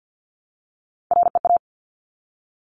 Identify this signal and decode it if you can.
Morse code, heard as a keyed tone, about 50 wpm, 700 Hz. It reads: REA
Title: Top Three Letter Combinations in Words 50wpm